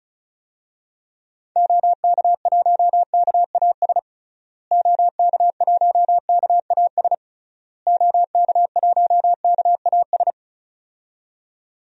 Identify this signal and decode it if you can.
Morse code, heard as a keyed tone, about 34 wpm, 700 Hz. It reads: OK1KAS OK1KAS OK1KAS
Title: DXCC-100 Call Signs - 6 Characters - Repeated 3x - 35wpm